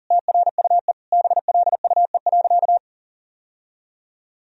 Morse code, heard as a keyed tone, about 40 wpm, 700 Hz. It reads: TRUE BLUE.
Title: Sentences from Top 1000 Words - Review Part 3 - 40wpm